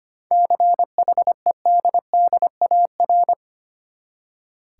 Morse code, heard as a keyed tone, about 25 wpm, 700 Hz. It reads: CHEDDAR
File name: practice / 25wpm / Easy Words with Hints - 25wpm